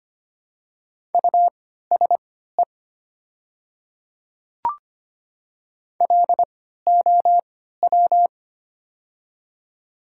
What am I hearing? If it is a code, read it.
Morse code, heard as a keyed tone, about 25 wpm, spaced as for 14 wpm, 700 Hz. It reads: USE LOW
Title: 3x Character Spacing - Top 300 Words - 25wpm